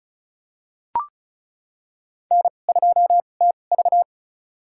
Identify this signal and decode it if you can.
Morse code, heard as a keyed tone, about 36 wpm, spaced as for 25 wpm, 700 Hz. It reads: N2TV
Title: USA Call Signs - 4 Characters - 2x Character Spacing - 35wpm